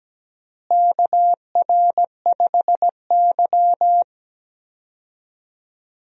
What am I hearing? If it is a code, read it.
Morse code, heard as a keyed tone, about 17 wpm, 700 Hz. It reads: KR5Y
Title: USA Call Signs - 4 Characters - Repeated 2x - 17wpm